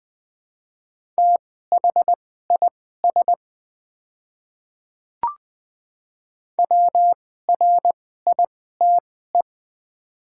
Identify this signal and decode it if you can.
Morse code, heard as a keyed tone, about 20 wpm, spaced as for 15 wpm, 700 Hz. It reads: THIS WRITE
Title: 2x Character Spacing - Top 100 Words - 20wpm